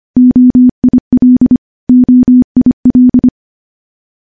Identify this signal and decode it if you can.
Morse code, heard as a keyed tone, about 25 wpm, 250 Hz. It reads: OIL OIL
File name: morse-training-25WPM